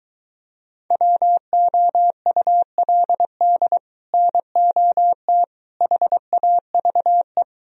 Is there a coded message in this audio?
Yes: Morse code from 0.9 to 7.4 s, heard as a keyed tone, about 22 wpm, 700 Hz. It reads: WOULD NOT HAVE